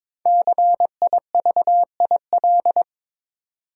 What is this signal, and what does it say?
Morse code, heard as a keyed tone, about 22 wpm, 700 Hz. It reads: CIVIL
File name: 5 Letter Words - First 3 Letters Given - 22wpm